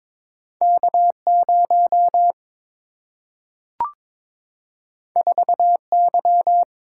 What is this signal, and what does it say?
Morse code, heard as a keyed tone, about 22 wpm, 700 Hz. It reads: K 0 4 Y
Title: Single-Letter Single-Number 22wpm